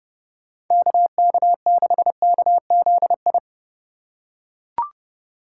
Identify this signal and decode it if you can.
Morse code, heard as a keyed tone, about 30 wpm, 700 Hz. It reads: KK6KZI
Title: USA Call Signs - 6 Characters - 4x Character Spacing - 30wpm